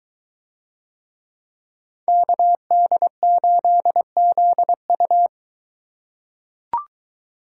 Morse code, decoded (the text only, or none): KD8ZU